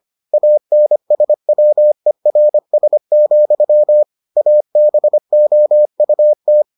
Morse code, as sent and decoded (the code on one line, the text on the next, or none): .- -. ... .-- . .-. ... --..-- / .- -... --- ..- -
ANSWERS, ABOUT